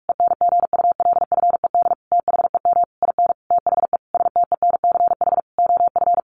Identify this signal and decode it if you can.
Morse code, heard as a keyed tone, about 44 wpm, 700 Hz. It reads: ENGULFED THEM IN THE STENCH OF
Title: A Sound Of Thunder 45wpm